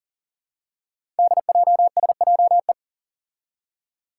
Morse code, heard as a keyed tone, about 42 wpm, 700 Hz. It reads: DJSJE
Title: Five Challenging Letter-Number Combinations 40wpm